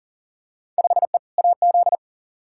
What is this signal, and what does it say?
Morse code, heard as a keyed tone, about 40 wpm, 700 Hz. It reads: 5E AZ
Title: Field Day Exchanges - 40wpm